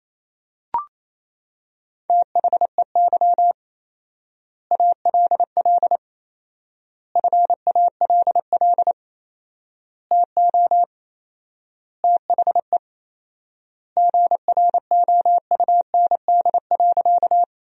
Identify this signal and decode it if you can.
Morse code, heard as a keyed tone, about 28 wpm, 700 Hz. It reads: THEY ALL FALL TO THE GROUND.
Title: 4x Word-Spacing - Sentences from Top 500 Words - 28wpm